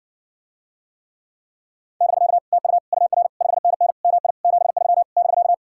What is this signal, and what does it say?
Morse code, heard as a keyed tone, about 60 wpm, 700 Hz. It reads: <BK> TU UR 5NN MI 73 <BK>